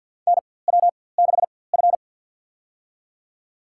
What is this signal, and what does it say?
Morse code, heard as a keyed tone, about 53 wpm, spaced as for 24 wpm, 700 Hz. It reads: NW6F
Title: USA Call Signs - 4x Character Spacing - 50wpm